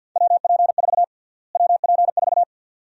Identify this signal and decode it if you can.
Morse code, heard as a keyed tone, about 51 wpm, 700 Hz. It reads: WP4 WP4
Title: USA Call Sign Prefixes - Repeated 2x - 50wpm